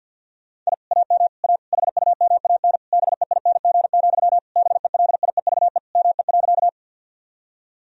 Morse code, heard as a keyed tone, about 50 wpm, 700 Hz. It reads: I AM A HUMAN BEING, BELIEVE ME.
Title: Sentences from Top 1000 Words - Review Part 3 - 50wpm